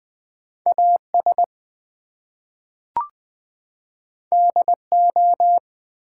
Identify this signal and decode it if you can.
Morse code, heard as a keyed tone, about 20 wpm, 700 Hz. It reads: AS DO
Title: Two Letter Words - 20wpm